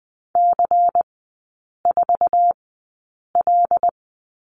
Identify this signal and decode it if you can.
Morse code, heard as a keyed tone, about 20 wpm, spaced as for 8 wpm, 700 Hz. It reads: C4L
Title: 093 Lesson 16e - Letter c - Three Characters 20wpm